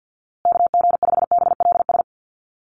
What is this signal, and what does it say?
Morse code, heard as a keyed tone, about 50 wpm, 700 Hz. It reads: KG5BLS
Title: USA Call Signs Repeated 2x 50wpm